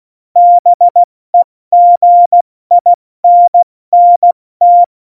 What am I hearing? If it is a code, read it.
Morse code, heard as a keyed tone, about 16 wpm, 700 Hz. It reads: BEGINNT
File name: Morsekurs / Witze / Witz 19 Mai, 2020 70 F80